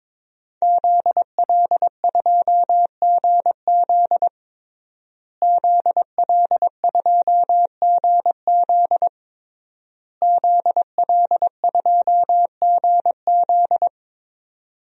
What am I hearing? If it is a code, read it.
Morse code, heard as a keyed tone, about 22 wpm, 700 Hz. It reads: ZL2GZ ZL2GZ ZL2GZ
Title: DXCC-100 Call Signs - 5 Characters - Repeated 3x - 22wpm